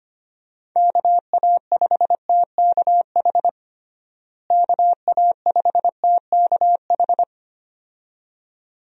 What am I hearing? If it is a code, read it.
Morse code, heard as a keyed tone, about 25 wpm, 700 Hz. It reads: KA5TKH KA5TKH
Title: USA Call Signs - 6 Characters - Repeated 2x - 25wpm